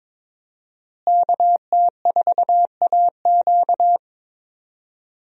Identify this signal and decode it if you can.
Morse code, heard as a keyed tone, about 22 wpm, 700 Hz. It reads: KT4AQ
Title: USA Call Signs - 5 Characters - Repeated 3x - 22wpm